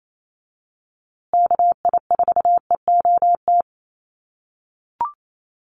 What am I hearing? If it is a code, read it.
Morse code, heard as a keyed tone, about 28 wpm, 700 Hz. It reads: KI4EOT
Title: USA Call Signs Repeated 3x 28wpm